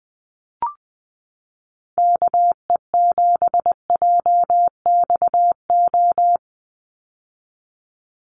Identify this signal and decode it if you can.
Morse code, heard as a keyed tone, about 20 wpm, 700 Hz. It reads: KE7JXO
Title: USA Call Signs 20wpm